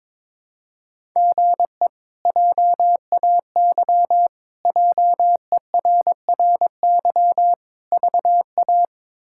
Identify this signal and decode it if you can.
Morse code, heard as a keyed tone, about 22 wpm, 700 Hz. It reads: GE JAY JERRY VA
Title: K1USN SST QSO Practice - 22wpm